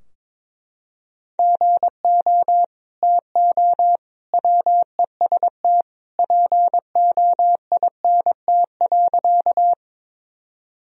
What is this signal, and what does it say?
Morse code, heard as a keyed tone, about 22 wpm, 700 Hz. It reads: GO TO WEST POINT.
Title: Sentences from Top 1000 Words - Review Part 2 - 22wpm